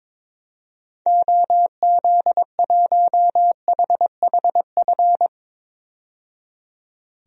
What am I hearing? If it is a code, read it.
Morse code, heard as a keyed tone, about 22 wpm, 700 Hz. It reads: OZ1HHF
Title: DXCC-100 Call Signs - 6 Characters - 22wpm